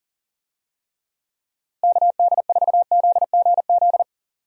Morse code, heard as a keyed tone, about 38 wpm, 700 Hz. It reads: KD4ZGZ
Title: USA Call Signs - 6 Characters - Repeated 3x - 40wpm